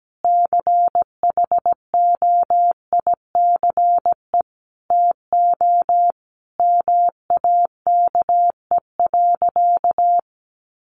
Morse code, heard as a keyed tone, about 17 wpm, 700 Hz. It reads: CHOICE TO MAKE.
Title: Sentences from Top 900 Words 17wpm